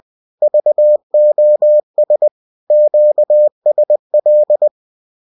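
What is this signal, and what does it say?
Morse code, heard as a keyed tone, about 20 wpm, 600 Hz. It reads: VOS QSL
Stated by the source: Bulletin sent in CW.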